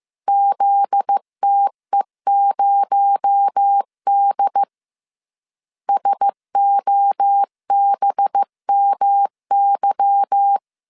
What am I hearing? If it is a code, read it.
Morse code, heard as a keyed tone, about 15 wpm, 800 Hz. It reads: ZTE0D SOBMY
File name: CW / DET / 1 9 1500C S C 14wpm 3245